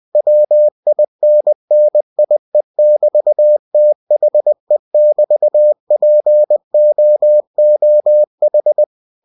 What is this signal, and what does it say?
Morse code, heard as a keyed tone, about 20 wpm, 600 Hz. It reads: WINNIE<BT>THE<BT>POOH
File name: cw-books / winnie-the-pooh / winnie-20wpm-page-0002